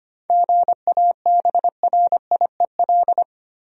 Morse code, heard as a keyed tone, about 25 wpm, 700 Hz. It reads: GABRIEL
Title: USA Common Names - 25wpm